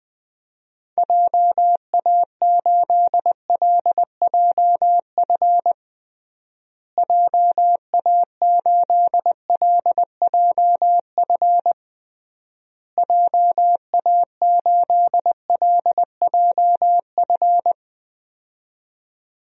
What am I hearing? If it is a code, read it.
Morse code, heard as a keyed tone, about 20 wpm, 700 Hz. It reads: JA8LJF JA8LJF JA8LJF
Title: DXCC-100 Call Signs - 6 Characters - Repeated 3x - 20wpm